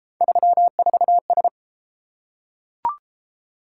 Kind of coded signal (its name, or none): Morse code